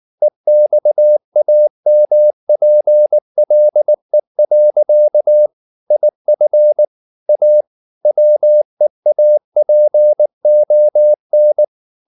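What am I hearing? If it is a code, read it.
Morse code, heard as a keyed tone, about 19 wpm, 600 Hz. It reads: EXAMPLE. IF A WEAPON
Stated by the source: Morse code at 19 wpm from CW med Gnister, reading Ken Follett's novel Never in English.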